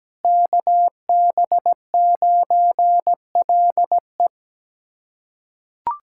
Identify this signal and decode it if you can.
Morse code, heard as a keyed tone, about 17 wpm, 700 Hz. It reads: KB9LE
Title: USA Callsign DE Callsign - 5 Characters - 17wpm